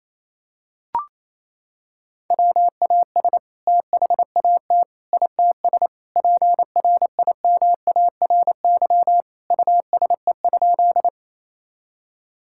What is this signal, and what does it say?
Morse code, heard as a keyed tone, about 28 wpm, 700 Hz. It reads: WAS THAT ITS PRIMARY USE?
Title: Sentences from Top 1000 Words - Review Part 5 - 28wpm